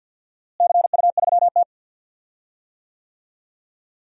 Morse code, heard as a keyed tone, about 52 wpm, 700 Hz. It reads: CU3T
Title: DXCC-100 Call Signs - 4 Characters - Repeated 3x - 50wpm